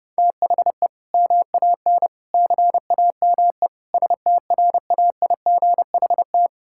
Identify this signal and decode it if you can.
Morse code, heard as a keyed tone, about 32 wpm, 700 Hz. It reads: THE MAN CAME STRAIGHT
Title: Sentences from Top 1000 Words - Review Part 6 - 30wpm